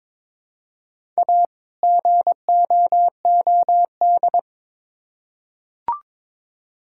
Morse code, heard as a keyed tone, about 22 wpm, 700 Hz. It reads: A GOOD
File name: Build A Phrase - 2-Word Phrases - 22wpm